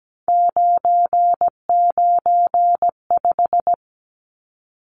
Morse code, heard as a keyed tone, about 17 wpm, 700 Hz. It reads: 995
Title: Numbers - 3-Digit 17wpm